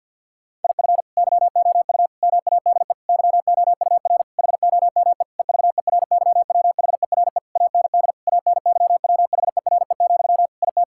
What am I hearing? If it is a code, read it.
Morse code, heard as a keyed tone, about 50 wpm, 700 Hz. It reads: IF YOU MADE YOUR HOME EVERYWHERE AND ANYWHERE, IT
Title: Sentences from Top 1600 Words 50wpm